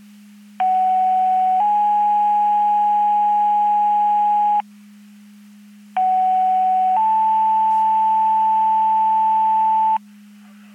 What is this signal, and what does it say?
Two-tone page, heard as a one-second tone, then a three-second tone, about 750 then 830 Hz, about 750 then 880 Hz. It is heard from 0.6 to 10.0 s.
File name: Portage 154.385 Paging